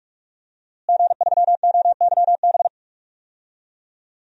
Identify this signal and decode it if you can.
Morse code, heard as a keyed tone, about 46 wpm, 700 Hz. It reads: G3OYB
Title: DXCC-100 Call Signs - 5 Characters - Repeated 2x - 45wpm